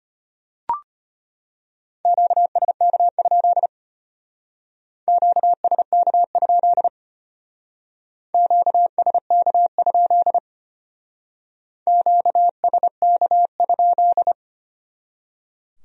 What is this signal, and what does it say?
Morse code, heard as a keyed tone, about 32 wpm, 700 Hz. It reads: QSK? QSK? QSK? QSK?